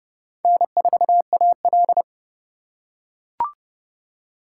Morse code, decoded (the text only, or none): N4AL